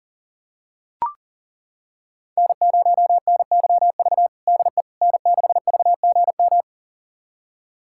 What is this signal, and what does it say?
Morse code, heard as a keyed tone, about 40 wpm, 700 Hz. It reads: N0NYV DE N6VGM
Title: USA Callsign DE Callsign - 5 Characters - 40wpm